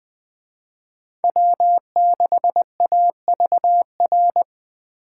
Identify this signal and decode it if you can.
Morse code, heard as a keyed tone, about 24 wpm, 700 Hz. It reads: W6AVR
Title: Speed Racing - USA 1x3 Call Signs 20wpm